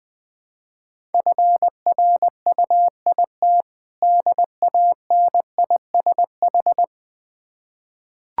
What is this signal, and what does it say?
Morse code, heard as a keyed tone, about 20 wpm, 700 Hz. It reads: FRUIT DANISH
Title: Difficult Words with Hints - 20wpm